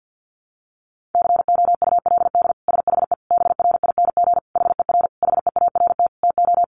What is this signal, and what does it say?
Morse code, heard as a keyed tone, about 50 wpm, 700 Hz. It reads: COULD SHE BRING HER HEART TO
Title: Sentences from Top 1400 Words 50wpm